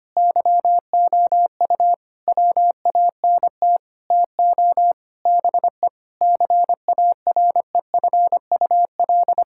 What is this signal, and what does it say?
Morse code, heard as a keyed tone, about 25 wpm, 700 Hz. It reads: YOU WANT TO BE CAREFUL
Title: Sentences from Top 1600 Words 25wpm